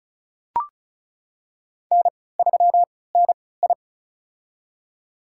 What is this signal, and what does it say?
Morse code, heard as a keyed tone, about 36 wpm, spaced as for 20 wpm, 700 Hz. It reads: N3NI
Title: USA Call Signs - 4 Characters - 3x Character Spacing - 35wpm